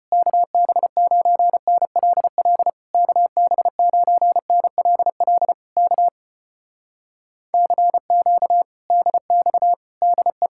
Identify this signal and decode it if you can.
Morse code, heard as a keyed tone, about 33 wpm, 700 Hz. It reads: KB9NLL KB9NLL K CQ DX DE